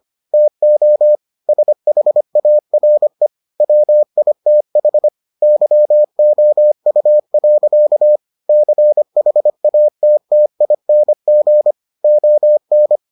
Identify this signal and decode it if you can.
Morse code, heard as a keyed tone, about 25 wpm, 600 Hz. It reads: TO SHARE WITH YOU. CHATTING ON